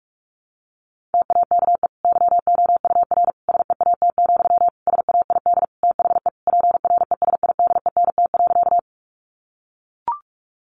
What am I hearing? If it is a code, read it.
Morse code, heard as a keyed tone, about 44 wpm, 700 Hz. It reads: TAKE YOUR SEAT, SAID THE PRESIDENT.
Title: Sentences from Top 1500 Words 45wpm